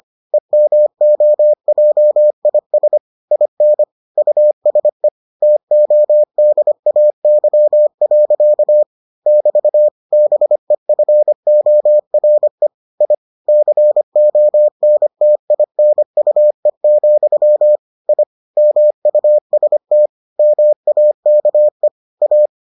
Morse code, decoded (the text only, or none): EMOJIS IN USE TODAY. <BT> BEFORE I CONTINUE, I MUST MAKE A